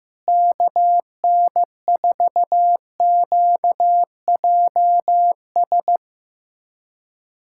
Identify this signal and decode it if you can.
Morse code, heard as a keyed tone, about 14 wpm, 700 Hz. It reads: KN4QJS
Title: USA Callsign DE Callsign 15wpm